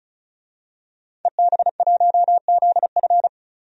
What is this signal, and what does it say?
Morse code, heard as a keyed tone, about 34 wpm, 700 Hz. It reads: EB1ZF
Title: DXCC-100 Call Signs - 5 Characters - 35wpm